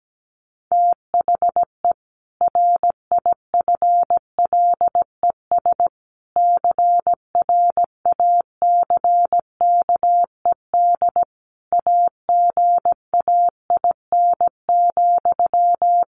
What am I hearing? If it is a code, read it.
Morse code, heard as a keyed tone, about 17 wpm, 700 Hz. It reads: THE RIFLES CRACKED AGAIN,
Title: A Sound Of Thunder 17wpm